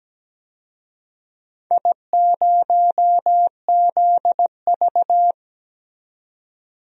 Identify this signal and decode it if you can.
Morse code, heard as a keyed tone, about 17 wpm, 700 Hz. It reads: I0ZV
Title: DXCC-100 Call Signs - 4 Characters - 17wpm